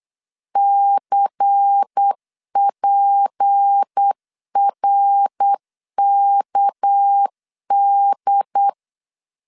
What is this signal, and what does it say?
Morse code, heard as a keyed tone, about 8 wpm, 800 Hz. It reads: CPRKD
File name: LEZIONI / 6 PKX0 / 6 PKX 8wpm 3041